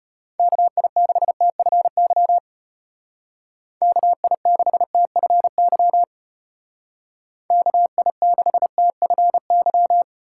Morse code, heard as a keyed tone, about 34 wpm, 700 Hz. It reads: KI6TFY KI6TFY KI6TFY